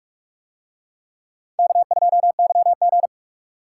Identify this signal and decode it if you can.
Morse code, heard as a keyed tone, about 43 wpm, 700 Hz. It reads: K2YG